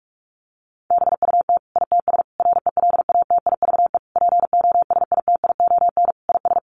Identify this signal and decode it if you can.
Morse code, heard as a keyed tone, about 44 wpm, 700 Hz. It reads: BUT ITS RELATIVE POSITION IS
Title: Sentences from Top 1400 Words 45wpm